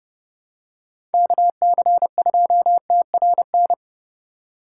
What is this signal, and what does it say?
Morse code, heard as a keyed tone, about 30 wpm, 700 Hz. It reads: KC2TRN
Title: USA Call Signs - 6 Characters - 4x Character Spacing - 30wpm